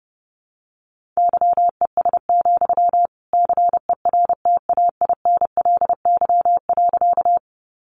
Morse code, heard as a keyed tone, about 30 wpm, 700 Hz. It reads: YES, CERTAINLY.
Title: Sentences from Top 900 Words 30wpm